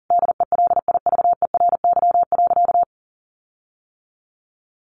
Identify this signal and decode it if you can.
Morse code, heard as a keyed tone, about 40 wpm, 700 Hz. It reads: DELIVERY.
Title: Sentences from Top 1500 Words 40wpm